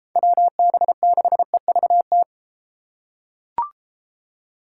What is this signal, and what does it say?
Morse code, heard as a keyed tone, about 32 wpm, 700 Hz. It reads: WB6EVT